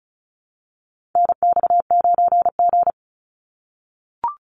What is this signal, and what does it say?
Morse code, heard as a keyed tone, about 34 wpm, 700 Hz. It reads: NX9G
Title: Call Signs - 35wpm